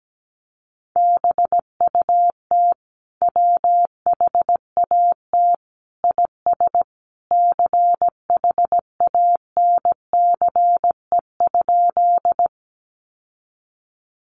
Morse code, heard as a keyed tone, about 17 wpm, 700 Hz. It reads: BUT WHAT IS CHANCE?
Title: Sentences from Top 800 Words 17wpm